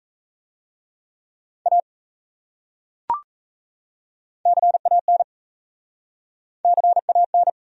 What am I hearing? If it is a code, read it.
Morse code, heard as a keyed tone, about 38 wpm, 700 Hz. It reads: A CAN CAN